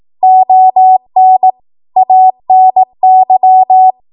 Morse code, heard as a keyed tone, about 18 wpm, 750 Hz. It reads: ON ANY
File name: Archive / 18 WPM / 18WPM